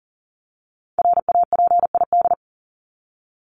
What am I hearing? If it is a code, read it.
Morse code, heard as a keyed tone, about 40 wpm, 700 Hz. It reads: RAPID
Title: LY Suffix - Top 100 Words with Base Words 40wpm